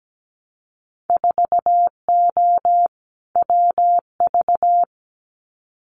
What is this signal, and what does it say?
Morse code, heard as a keyed tone, about 17 wpm, 700 Hz. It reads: 4O WV
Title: Winter Field Day Exchanges Encore 17wpm